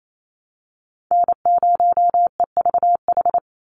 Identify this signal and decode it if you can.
Morse code, heard as a keyed tone, about 28 wpm, 700 Hz. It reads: N0EVH
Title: NA SOTA Chaser Callsigns 28wpm